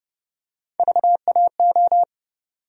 Morse code, heard as a keyed tone, about 30 wpm, 700 Hz. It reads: VAO
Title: USA Call Sign Postfixes - 3 Letters 30wpm